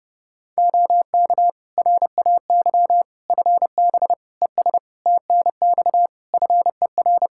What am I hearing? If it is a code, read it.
Morse code, heard as a keyed tone, about 30 wpm, 700 Hz. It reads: OK RAY FB ES TNX FER
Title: Practice QSO - Exchange 2 - Info - 30wpm